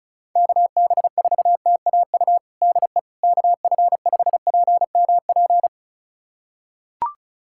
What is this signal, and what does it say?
Morse code, heard as a keyed tone, about 36 wpm, 700 Hz. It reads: KB4TAU DE KF5PMP
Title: USA Callsign DE Callsign - 6 Characters - 35wpm